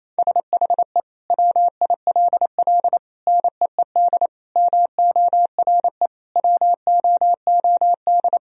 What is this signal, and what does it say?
Morse code, heard as a keyed tone, about 28 wpm, 700 Hz. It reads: SHE WILL NEED MORE WOOD
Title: Sentences from Top 1000 Words - Review Part 2 - 28wpm